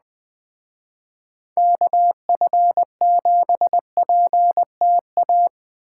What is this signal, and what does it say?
Morse code, heard as a keyed tone, about 19 wpm, 700 Hz. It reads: KF7PTA